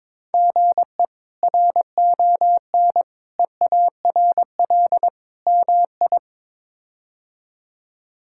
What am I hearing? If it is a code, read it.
Morse code, heard as a keyed tone, about 22 wpm, 700 Hz. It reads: GE RON EARL MI
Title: K1USN SST QSO Practice - 22wpm